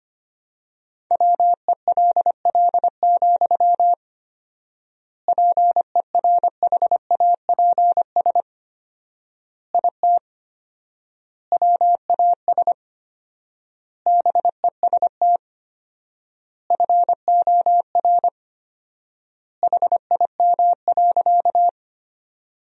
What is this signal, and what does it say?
Morse code, heard as a keyed tone, about 25 wpm, 700 Hz. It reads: WELL, PERHAPS IT WAS BEST FOR HIM.
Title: 4x Word-Spacing - Sentences from Top 500 Words - 25wpm